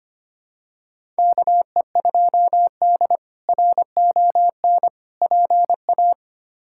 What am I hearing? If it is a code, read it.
Morse code, heard as a keyed tone, about 25 wpm, 700 Hz. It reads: KE2D RON PA
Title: K1USN SST QSO Practice - 25wpm